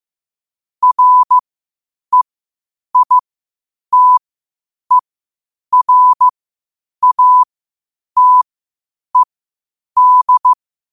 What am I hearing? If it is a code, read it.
Morse code, heard as a keyed tone, about 15 wpm, spaced as for 8 wpm, 1000 Hz. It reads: REITERATED